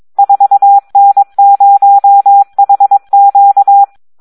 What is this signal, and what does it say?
Morse code, heard as a keyed tone, about 22 wpm, 800 Hz. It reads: 4N0HQ
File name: Audio practica CW / SD 1 22